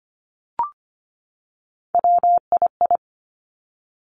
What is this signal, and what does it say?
Morse code, heard as a keyed tone, about 25 wpm, 700 Hz. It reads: WII
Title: Three Challenging Letter-Number Combinations 25wpm